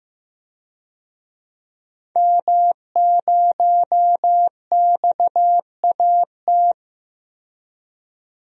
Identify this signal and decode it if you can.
Morse code, heard as a keyed tone, about 15 wpm, 700 Hz. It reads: M0XAT
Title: DXCC-100 Call Signs - 5 Characters - 15wpm